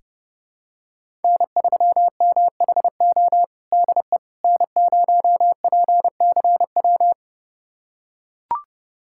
Morse code, heard as a keyed tone, about 29 wpm, 700 Hz. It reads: N3MHO DE N0PCW
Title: USA Callsign DE Callsign - 5 Characters - 30wpm